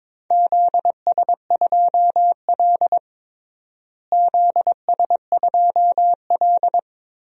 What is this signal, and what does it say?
Morse code, heard as a keyed tone, about 22 wpm, 700 Hz. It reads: ZS2L ZS2L
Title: DXCC-100 Call Signs - 4 Characters - Repeated 3x - 22wpm